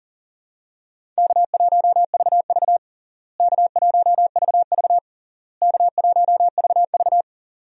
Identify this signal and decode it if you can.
Morse code, heard as a keyed tone, about 40 wpm, 700 Hz. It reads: K1VV K1VV K1VV
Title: USA Call Signs - 4 Characters - Repeated 3x - 40wpm